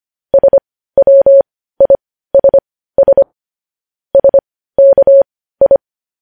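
Morse code, heard as a keyed tone, about 25 wpm, spaced as for 15 wpm, 550 Hz. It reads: SWISS SKI
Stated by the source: Morse news headlines practice sent at 15 WPM.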